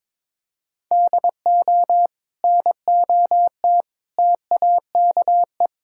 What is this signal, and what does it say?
Morse code, heard as a keyed tone, about 21 wpm, 700 Hz. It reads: DO NOT TAKE
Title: Sentences from Top 1600 Words 22wpm